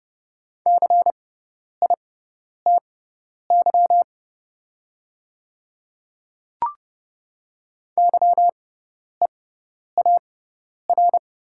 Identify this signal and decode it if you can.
Morse code, heard as a keyed tone, about 30 wpm, spaced as for 10 wpm, 690 Hz. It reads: CITY YEAR
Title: 6x Character Spacing - Top 500 Words - 30wpm